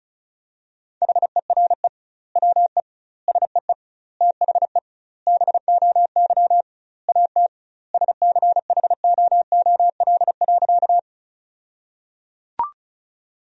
Morse code, heard as a keyed tone, about 36 wpm, 700 Hz. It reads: HERE WE SEE THE BOY AT SCHOOL.
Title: 2x Word-Spacing - Sentences from Top 300 Words - 35wpm